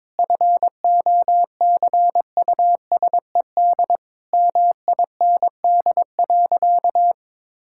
Morse code, heard as a keyed tone, about 21 wpm, 700 Hz. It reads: FOCUSED MIND.
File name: Difficult Sentences with Hints - 22wpm